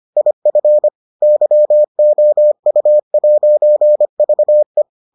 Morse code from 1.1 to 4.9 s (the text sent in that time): YOU'VE